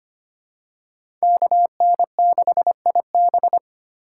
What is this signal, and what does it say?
Morse code, heard as a keyed tone, about 25 wpm, 700 Hz. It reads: KN6IB
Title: USA Call Signs - 5 Characters - 4x Character Spacing - 25wpm